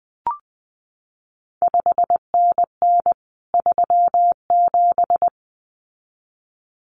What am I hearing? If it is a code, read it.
Morse code, heard as a keyed tone, about 20 wpm, 700 Hz. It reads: 5NN 37
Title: CQ WW DX Contest Exchange Encore - 20wpm